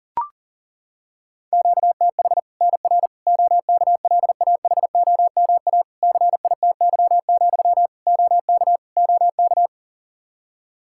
Morse code, heard as a keyed tone, about 40 wpm, 700 Hz. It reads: QTH NR OKLAHOMA CITY, OK OK
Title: Practice QSO - Exchange 1 - Beginning - 40wpm